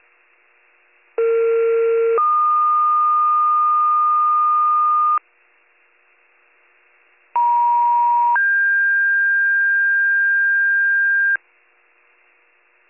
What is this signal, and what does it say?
Two-tone page, heard as a one-second tone, then a three-second tone, about 460 then 1220 Hz, about 950 then 1600 Hz. It is heard from 1.2 to 11.4 s.